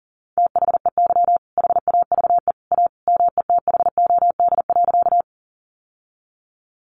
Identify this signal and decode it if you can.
Morse code, heard as a keyed tone, about 40 wpm, 700 Hz. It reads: THEY HAVE A METHOD.
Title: Sentences from Top 600 Words 40wpm